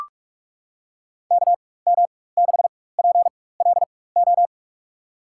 Morse code, heard as a keyed tone, about 44 wpm, spaced as for 21 wpm, 700 Hz. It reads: KM6PLO
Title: USA Call Signs - 4x Character Spacing - 45wpm